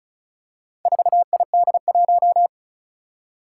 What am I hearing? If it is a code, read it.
Morse code, heard as a keyed tone, about 36 wpm, 700 Hz. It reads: 4ID1
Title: Four Challenging Letter-Number Combinations 35wpm